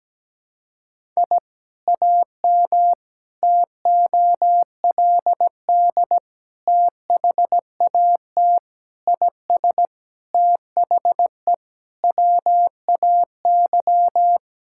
Morse code, heard as a keyed tone, about 17 wpm, 700 Hz. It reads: I AM TOLD THAT IS THE WAY
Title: Sentences from Top 1000 Words - Review Part 5 - 17wpm